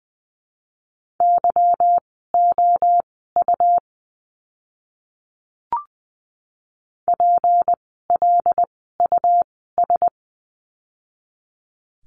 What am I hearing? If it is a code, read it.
Morse code, heard as a keyed tone, about 20 wpm, spaced as for 14 wpm, 700 Hz. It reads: YOU PLUS
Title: 153 Lesson 25f - Letter p - Words 20wpm